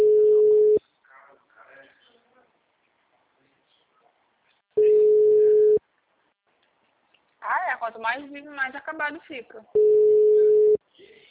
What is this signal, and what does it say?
Ringback tone (Europe, 425 Hz), with speech.